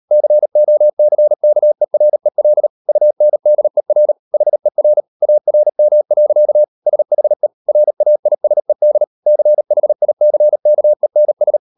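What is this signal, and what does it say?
Morse code, heard as a keyed tone, about 38 wpm, 600 Hz. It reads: COCKEREL UNDER HER ARM. SHE RAISED CHICKENS
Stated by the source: Morse code practice at 38 WPM, sending Ken Follett's novel Never in English, from CW med Gnister.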